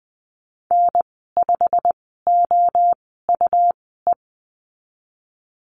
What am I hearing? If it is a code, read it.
Morse code, heard as a keyed tone, about 20 wpm, spaced as for 15 wpm, 700 Hz. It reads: N5OUE
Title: 128 Lesson 21g - Number 6 - Call Signs 20wpm